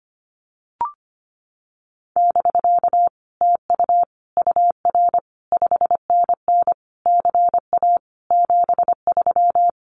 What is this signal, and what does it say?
Morse code, heard as a keyed tone, about 25 wpm, 700 Hz. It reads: <BK> TU UR 5NN CA 73
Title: Common QSO Phrases Encore 25wpm